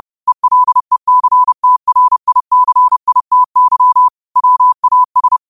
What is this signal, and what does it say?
Morse code, heard as a keyed tone, about 30 wpm, 1000 Hz. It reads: ELECTRICITY WAS